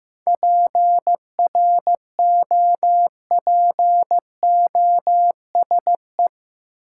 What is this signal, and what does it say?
Morse code, heard as a keyed tone, about 15 wpm, 700 Hz. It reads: PROPOSE
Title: Seven Letter Words - 15wpm